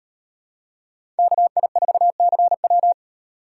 Morse code, heard as a keyed tone, about 38 wpm, 700 Hz. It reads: KI4CW